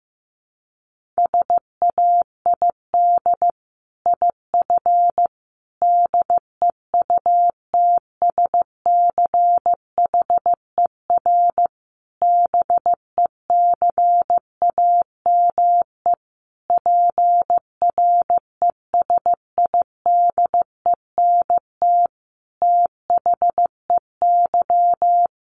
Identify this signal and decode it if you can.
Morse code, heard as a keyed tone, about 15 wpm, 700 Hz. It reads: SAID IF DEUTSCHER BECAME PRESIDENT THEY
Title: A Sound Of Thunder 15wpm